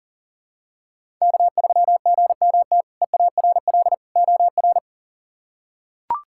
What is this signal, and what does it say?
Morse code, heard as a keyed tone, about 40 wpm, 690 Hz. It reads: K3GMT EARL OR
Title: K1USN SST QSO Practice - 40wpm